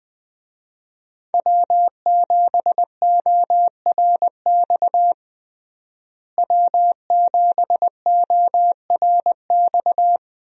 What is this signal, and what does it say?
Morse code, heard as a keyed tone, about 20 wpm, 700 Hz. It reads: W7ORX W7ORX
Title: USA Call Signs - 5 Characters - Repeated 2x - 20wpm